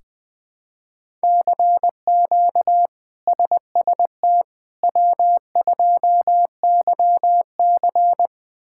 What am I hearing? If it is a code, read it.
Morse code, heard as a keyed tone, about 20 wpm, 700 Hz. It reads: CQ SST W2YC
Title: K1USN SST QSO Practice - 20wpm